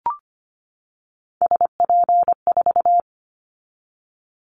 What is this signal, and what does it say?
Morse code, heard as a keyed tone, about 25 wpm, 700 Hz. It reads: SP4
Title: Callsign Prefixes for Top 25 DXCC - 25wpm